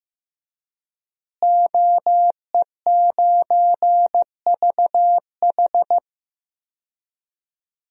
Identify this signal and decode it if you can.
Morse code, heard as a keyed tone, about 14 wpm, 700 Hz. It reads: OE9VH
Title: DXCC-100 Call Signs - 5 Characters - Repeated 3x - 15wpm